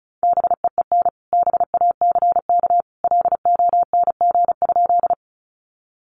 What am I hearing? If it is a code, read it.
Morse code, heard as a keyed tone, about 34 wpm, 700 Hz. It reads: BEEN BACK LONG?
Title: Sentences from Top 200 Words 35wpm